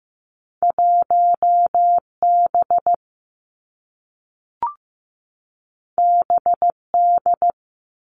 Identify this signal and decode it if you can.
Morse code, heard as a keyed tone, about 15 wpm, 700 Hz. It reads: 1 B B D
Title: Two Challenging Letter-Number Combinations 15wpm